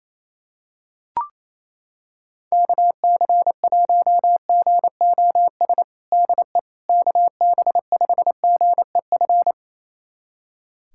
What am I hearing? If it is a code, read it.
Morse code, heard as a keyed tone, about 28 wpm, 700 Hz. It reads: KC1GOS DE KB5GEF
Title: USA Callsign DE Callsign - 6 Characters - 28wpm